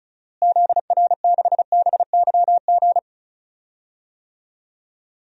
Morse code, heard as a keyed tone, about 36 wpm, 700 Hz. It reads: ZR6BYG
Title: DXCC-100 Call Signs - 6 Characters - Repeated 2x - 35wpm